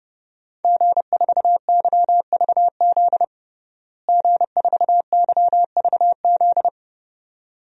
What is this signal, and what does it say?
Morse code, heard as a keyed tone, about 32 wpm, 700 Hz. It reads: G4YVZ G4YVZ
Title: DXCC-100 Call Signs - 5 Characters - Repeated 2x - 30wpm